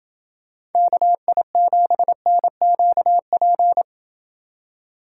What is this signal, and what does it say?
Morse code, heard as a keyed tone, about 28 wpm, 700 Hz. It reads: KI7NQP